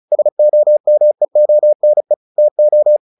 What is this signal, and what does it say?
Morse code, heard as a keyed tone, about 34 wpm, 600 Hz. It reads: SOMEONE TO
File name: the old man and the sea 2 35wpm